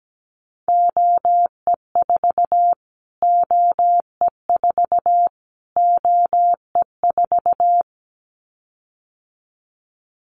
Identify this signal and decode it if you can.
Morse code, heard as a keyed tone, about 17 wpm, 700 Hz. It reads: OE4 OE4 OE4
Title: Callsign Prefixes for Top 25 DXCC - Repeated 3x - 17wpm